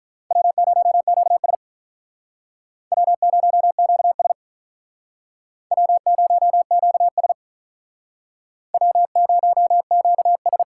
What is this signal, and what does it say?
Morse code, heard as a keyed tone, about 48 wpm, 700 Hz. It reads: W0QS W0QS W0QS W0QS